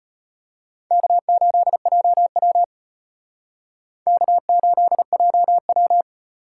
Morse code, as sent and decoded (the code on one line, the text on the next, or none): -.- ---.. .--- .-- / -.- ---.. .--- .--
K8JW K8JW